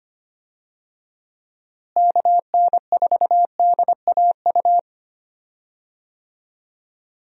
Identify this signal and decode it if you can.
Morse code, heard as a keyed tone, about 25 wpm, 700 Hz. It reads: KN4DAU
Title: USA Call Signs - 6 Characters - Repeated 2x - 25wpm